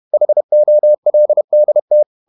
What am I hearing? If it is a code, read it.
Morse code, heard as a keyed tone, about 31 wpm, 600 Hz. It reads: HOLDT
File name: Ferie QRQ 31wpm